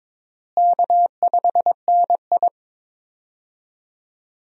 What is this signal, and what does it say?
Morse code, heard as a keyed tone, about 22 wpm, 700 Hz. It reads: K5NI
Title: USA Call Signs - 4 Characters - Repeated 3x - 22wpm